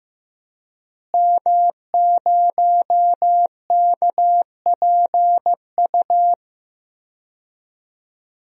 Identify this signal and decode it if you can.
Morse code, heard as a keyed tone, about 15 wpm, 700 Hz. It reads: M0KPU
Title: DXCC-100 Call Signs - 5 Characters - Repeated 3x - 15wpm